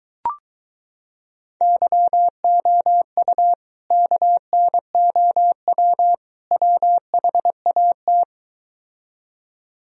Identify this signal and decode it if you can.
Morse code, heard as a keyed tone, about 23 wpm, 700 Hz. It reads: YOU KNOW WHAT